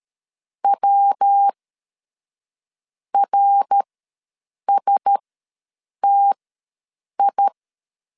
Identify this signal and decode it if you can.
Morse code, heard as a keyed tone, about 13 wpm, spaced as for 6 wpm, 800 Hz. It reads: WRSTI
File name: LEZIONI / 5 LBVC / 5 LBVC ws 12wpm 3159